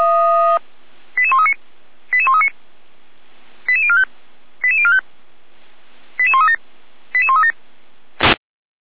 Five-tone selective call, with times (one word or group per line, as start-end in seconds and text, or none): 1.2-1.5 88139
2.1-2.5 88139
3.7-4.0 88045
4.6-5.0 88045
6.2-6.6 88137
7.1-7.5 88137